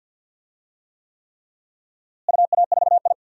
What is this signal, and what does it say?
Morse code, heard as a keyed tone, about 51 wpm, 700 Hz. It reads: UA4I
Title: DXCC-100 Call Signs - 4 Characters - Repeated 3x - 50wpm